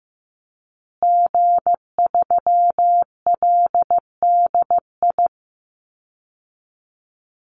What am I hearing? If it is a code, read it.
Morse code, heard as a keyed tone, about 15 wpm, 700 Hz. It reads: G3LDI
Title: Call Signs Repeated 2x - 15wpm